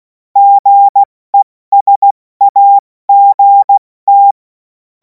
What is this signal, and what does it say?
Morse code, heard as a keyed tone, about 16 wpm, spaced as for 14 wpm, 800 Hz. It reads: GESAGT
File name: Morsekurs / Witze / Witz 10. März 2026 70 F80